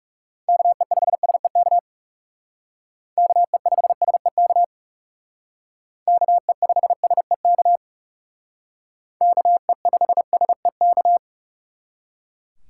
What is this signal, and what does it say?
Morse code, heard as a keyed tone, about 38 wpm, 700 Hz. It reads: KE5SEK KE5SEK KE5SEK KE5SEK